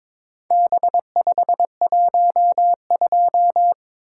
Morse code, heard as a keyed tone, about 22 wpm, 700 Hz. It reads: B512
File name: practice / 22wpm / Four Letter-Number Combinations - 22wpm